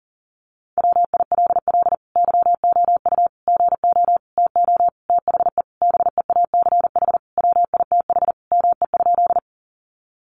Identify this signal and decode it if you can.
Morse code, heard as a keyed tone, about 40 wpm, 700 Hz. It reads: WILL YOU GO TO THE BEACH WITH ME?
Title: Sentences from Top 1500 Words 40wpm